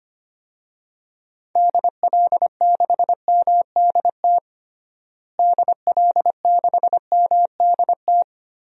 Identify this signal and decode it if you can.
Morse code, heard as a keyed tone, about 25 wpm, 700 Hz. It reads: DL6MDT DL6MDT
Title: DXCC-100 Call Signs - 6 Characters - Repeated 2x - 25wpm